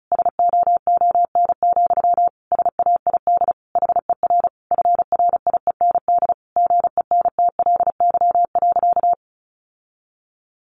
Morse code, heard as a keyed tone, about 34 wpm, 700 Hz. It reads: SOON, SAID HER FRIEND GENTLY.
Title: Sentences from Top 1500 Words 35wpm